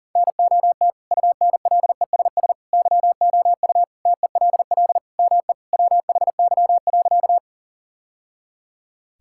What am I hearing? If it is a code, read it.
Morse code, heard as a keyed tone, about 37 wpm, 700 Hz. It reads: NOT UNLESS YOU TELL ME WHY.
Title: Sentences from Top 1000 Words - Review Part 6 - 40wpm